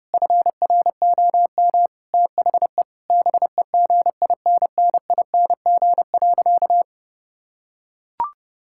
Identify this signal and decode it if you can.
Morse code, heard as a keyed tone, about 32 wpm, 700 Hz. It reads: FROM THE BEGINNING.
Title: Sentences from Top 1000 Words - Review Part 5 - 30wpm